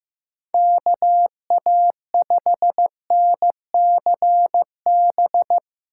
Morse code, heard as a keyed tone, about 15 wpm, 700 Hz. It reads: KA5NCB
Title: USA Call Signs - 6 Characters - 3x Character Spacing - 15wpm